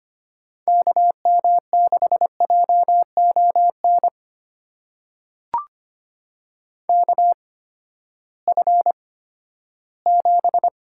Morse code, heard as a keyed tone, about 25 wpm, spaced as for 14 wpm, 700 Hz. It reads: KM6JON KF7
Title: USA Call Signs - 6 Characters - 8x Character Spacing - 25wpm